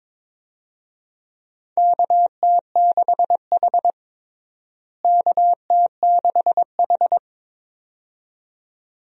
Morse code, heard as a keyed tone, about 22 wpm, 700 Hz. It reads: KT6H KT6H
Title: USA Call Signs - 4 Characters - Repeated 2x - 22wpm